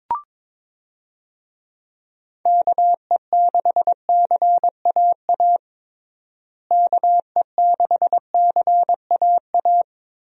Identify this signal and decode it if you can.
Morse code, heard as a keyed tone, about 22 wpm, 700 Hz. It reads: KE6CAA KE6CAA
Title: USA Call Signs - 6 Characters - Repeated 3x - 22wpm